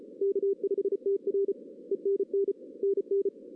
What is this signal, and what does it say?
Morse code, heard as a keyed tone, about 33 wpm, 440 Hz. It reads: K5TR E NN NN
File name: 2016 arrl dx cw / 2016-ARRL-DX-CW-K5TR- - 0-3